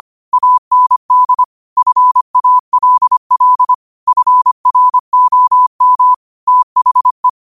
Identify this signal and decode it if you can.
Morse code, heard as a keyed tone, about 25 wpm, 1000 Hz. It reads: AND FALL FROM THE